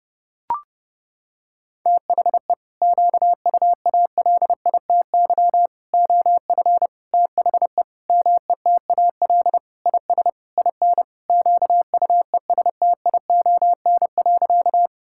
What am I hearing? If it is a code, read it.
Morse code, heard as a keyed tone, about 30 wpm, 700 Hz. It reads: THE QUALITY OF THE METAL IS IN QUESTION.
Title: Sentences from Top 1000 Words - Review Part 6 - 30wpm